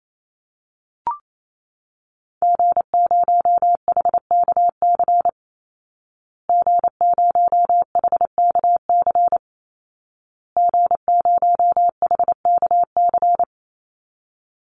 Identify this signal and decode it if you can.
Morse code, heard as a keyed tone, about 28 wpm, 700 Hz. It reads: G0HKC G0HKC G0HKC
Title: Call Signs Repeated 2x - 28wpm